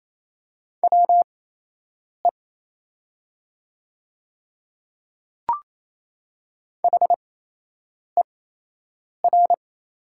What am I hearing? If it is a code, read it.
Morse code, heard as a keyed tone, about 28 wpm, spaced as for 8 wpm, 700 Hz. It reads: WE HER
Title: 8x Character Spacing - Top 100 Words - 28wpm